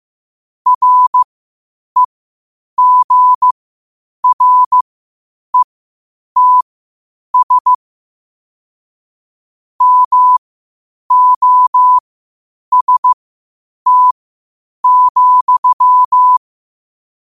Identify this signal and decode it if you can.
Morse code, heard as a keyed tone, about 15 wpm, spaced as for 8 wpm, 1000 Hz. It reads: REGRETS MOST,
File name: qotdfiles / 8wpm